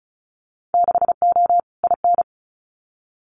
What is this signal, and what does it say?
Morse code, heard as a keyed tone, about 34 wpm, 700 Hz. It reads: 6O IN
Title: Winter Field Day Exchanges Encore 35wpm